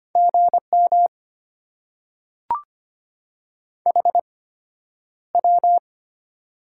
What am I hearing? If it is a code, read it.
Morse code, heard as a keyed tone, about 25 wpm, spaced as for 11 wpm, 690 Hz. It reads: GM HW
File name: Top 90 QSO Elements - 8x Character Spacing - 25wpm